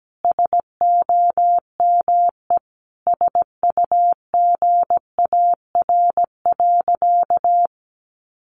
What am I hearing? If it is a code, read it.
Morse code, heard as a keyed tone, about 17 wpm, 700 Hz. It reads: SOME SUGAR.
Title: Sentences from Top 1200 Words 17wpm